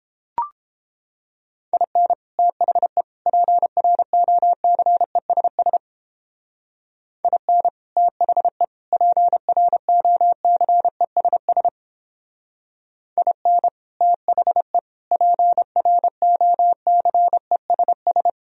Morse code, heard as a keyed tone, about 29 wpm, 700 Hz. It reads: IN THE PROCESS IN THE PROCESS IN THE PROCESS